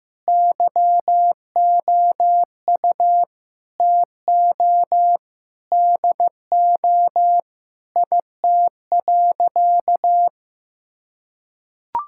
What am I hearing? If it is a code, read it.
Morse code, heard as a keyed tone, about 16 wpm, 700 Hz. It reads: YOU TO DO IT.
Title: Sentences from Top 1000 Words - Review Part 5 - 15wpm